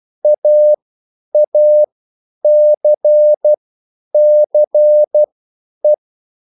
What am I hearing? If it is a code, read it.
Morse code, heard as a keyed tone, about 12 wpm, spaced as for 9 wpm, 600 Hz. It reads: AACCE